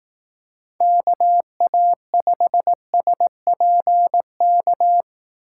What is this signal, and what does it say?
Morse code, heard as a keyed tone, about 18 wpm, 700 Hz. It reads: KA5SPK
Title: Speed Racing - USA 2x3 Call Signs 15wpm